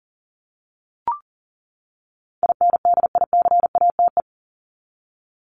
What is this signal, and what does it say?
Morse code, heard as a keyed tone, about 40 wpm, 700 Hz. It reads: INDICATE
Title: Top 1251-1500 Words 40wpm